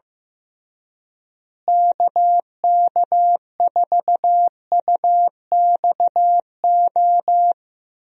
Morse code, heard as a keyed tone, about 15 wpm, 700 Hz. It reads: KK4UXO
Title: USA Call Signs - 6 Characters - Repeated 3x - 15wpm